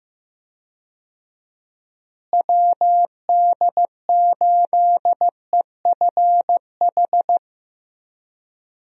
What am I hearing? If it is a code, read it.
Morse code, heard as a keyed tone, about 15 wpm, 700 Hz. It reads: WD8EFH
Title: USA Call Signs - 6 Characters - Repeated 3x - 15wpm